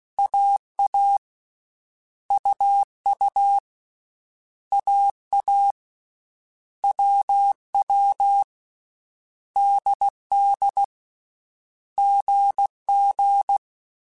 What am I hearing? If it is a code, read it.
Morse code, heard as a keyed tone, about 16 wpm, 800 Hz. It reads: AA UU AA WW DD GG
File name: Basic Sets / Basic Letter Groups Collective7 / AWUNDG@13